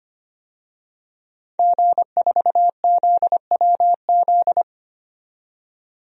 Morse code, heard as a keyed tone, about 25 wpm, 700 Hz. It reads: G4ZWZ
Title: DXCC-100 Call Signs - 5 Characters - 25wpm